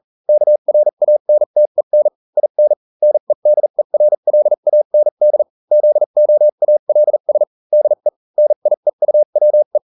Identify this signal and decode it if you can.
Morse code, heard as a keyed tone, about 40 wpm, spaced as for 35 wpm, 600 Hz. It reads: KRANTEN IN NEDERLAND ZOALS DE NIEUWE